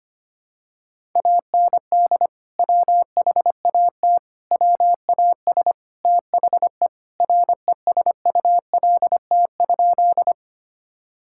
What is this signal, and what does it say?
Morse code, heard as a keyed tone, about 25 wpm, 700 Hz. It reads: AND WHAT WAS THE RESULT?
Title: Sentences from Top 1000 Words - Review Part 5 - 25wpm